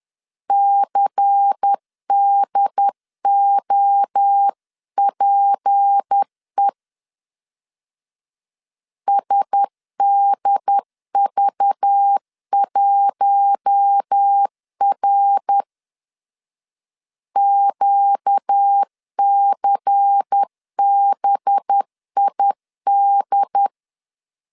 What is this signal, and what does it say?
Morse code, heard as a keyed tone, about 10 wpm, 800 Hz. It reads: CDOPE SDV1R QCBID
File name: LEZIONI / 6 PKX0 / 6 P 10wpm 2432